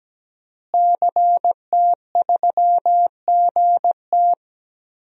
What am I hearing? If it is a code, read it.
Morse code, heard as a keyed tone, about 17 wpm, 700 Hz. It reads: CT3GT
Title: DXCC-100 Call Signs - 5 Characters - Repeated 2x - 17wpm